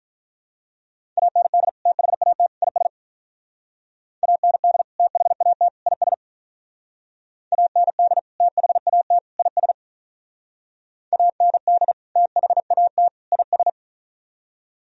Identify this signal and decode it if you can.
Morse code, heard as a keyed tone, about 44 wpm, 700 Hz. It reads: AND THAT IS AND THAT IS AND THAT IS AND THAT IS